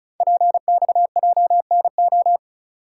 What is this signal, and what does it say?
Morse code, heard as a keyed tone, about 36 wpm, 690 Hz. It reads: PXJNO
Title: Code Groups - 5 Letters - 35wpm